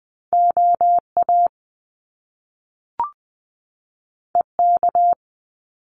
Morse code, heard as a keyed tone, about 20 wpm, 700 Hz. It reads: OA EK
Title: Top Two Letter Combinations in Words 20wpm